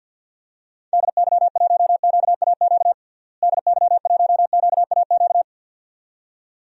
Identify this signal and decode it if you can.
Morse code, heard as a keyed tone, about 46 wpm, 700 Hz. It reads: DY1QAQ DY1QAQ
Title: DXCC-100 Call Signs - 6 Characters - Repeated 2x - 50wpm